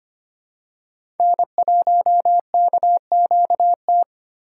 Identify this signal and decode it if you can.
Morse code, heard as a keyed tone, about 25 wpm, 700 Hz. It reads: N1KQT
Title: USA Call Signs - 8x Character Spacing - 25wpm